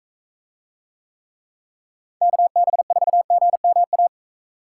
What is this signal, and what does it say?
Morse code, heard as a keyed tone, about 42 wpm, 700 Hz. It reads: KB4GMA